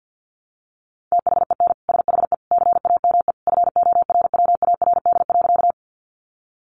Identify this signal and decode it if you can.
Morse code, heard as a keyed tone, about 50 wpm, 700 Hz. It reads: THEN SHE CAME FORWARD.
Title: Sentences from Top 1300 Words 50wpm